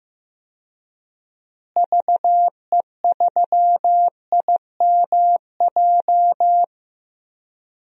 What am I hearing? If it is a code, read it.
Morse code, heard as a keyed tone, about 16 wpm, 700 Hz. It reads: VE3IMJ
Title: DXCC-100 Call Signs - 6 Characters - Repeated 2x - 15wpm